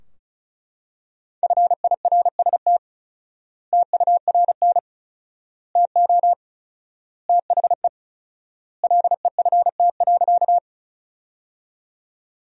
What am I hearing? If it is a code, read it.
Morse code, heard as a keyed tone, about 36 wpm, 700 Hz. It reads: FIRST TURN TO THE LEFT.
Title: 4x Word-Spacing - Sentences from Top 300 Words - 35wpm